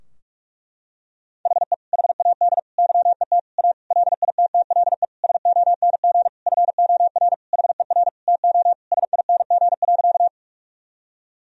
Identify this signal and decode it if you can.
Morse code, heard as a keyed tone, about 46 wpm, 700 Hz. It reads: HE HAD YET A LITTLE SONG FOR HER TO SING.
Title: Sentences from Top 1000 Words - Review Part 2 - 45wpm